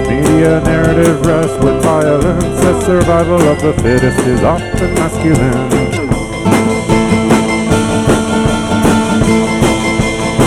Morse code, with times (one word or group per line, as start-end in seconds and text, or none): none